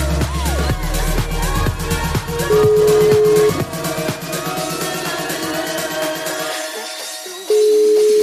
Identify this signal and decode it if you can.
Ringback tone (Europe, 425 Hz), with noise.